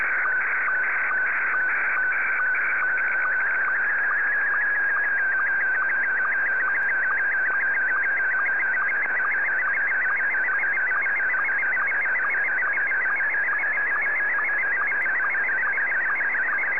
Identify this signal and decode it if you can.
SSTV (slow-scan television), Scottie 1.